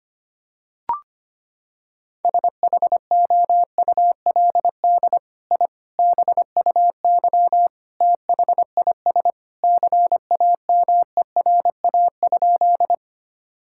Morse code, heard as a keyed tone, about 25 wpm, 700 Hz. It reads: SHOULD I BUY THIS CAMERA?
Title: Sentences from Top 1000 Words - Review Part 5 - 25wpm